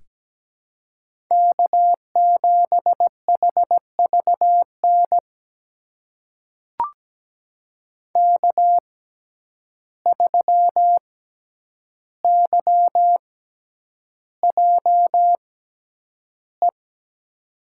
Morse code, heard as a keyed tone, about 17 wpm, spaced as for 9 wpm, 700 Hz. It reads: K7HVN K3YJE
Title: USA Call Signs - 5 Characters - 6x Character Spacing - 17wpm